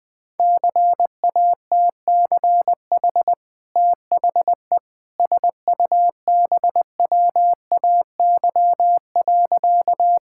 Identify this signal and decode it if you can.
Morse code, heard as a keyed tone, about 20 wpm, 700 Hz. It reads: CATCH THE SUBWAY.
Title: Easy Sentences with Hints - 20wpm